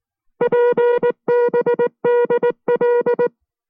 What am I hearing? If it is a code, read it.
Morse code, heard as a keyed tone, about 19 wpm, 480 Hz. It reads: PBDL